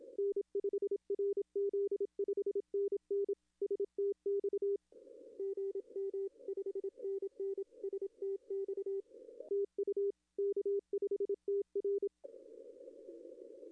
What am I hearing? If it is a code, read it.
Morse code, heard as a keyed tone, about 27 wpm, 400 Hz. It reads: N5RZ 5NN STX GM 5NN STX TU K5TR
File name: contests / arrl-160-k5tr-op-w5tn -0600